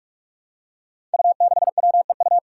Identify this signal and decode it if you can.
Morse code, heard as a keyed tone, about 46 wpm, 700 Hz. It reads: U6WEU
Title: Five Challenging Letter-Number Combinations 45wpm